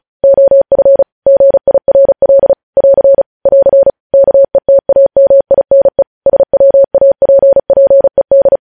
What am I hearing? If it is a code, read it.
Morse code, heard as a keyed tone, about 36 wpm, 550 Hz. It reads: OF GIRL <AR> <AR> KETAMINE SWAPPED
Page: news headlines in morse code at 35 wpm